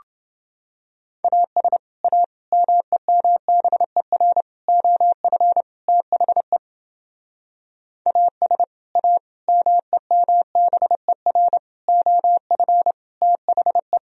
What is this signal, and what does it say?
Morse code, heard as a keyed tone, about 29 wpm, 700 Hz. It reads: AS A MEMBER OF THE AS A MEMBER OF THE